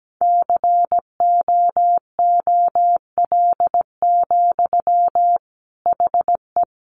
Morse code, heard as a keyed tone, about 17 wpm, 700 Hz. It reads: COOL, HE
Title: Sentences from Top 1200 Words 17wpm